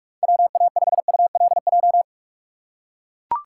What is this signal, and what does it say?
Morse code, heard as a keyed tone, about 44 wpm, 700 Hz. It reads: WA5ULJ